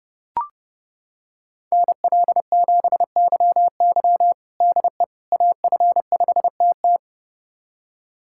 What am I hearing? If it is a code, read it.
Morse code, heard as a keyed tone, about 30 wpm, 700 Hz. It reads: NL7YY DE AF5TT
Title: USA Callsign DE Callsign - 5 Characters - 30wpm